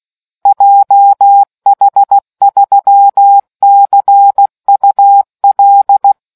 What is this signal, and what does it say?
Morse code, heard as a keyed tone, about 16 wpm, 800 Hz. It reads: JH3CUL